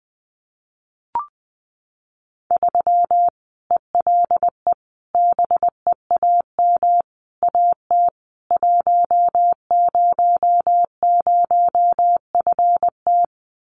Morse code, heard as a keyed tone, about 20 wpm, 700 Hz. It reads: 3 ELE BEAM AT 100FT
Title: Common QSO Phrases 20wpm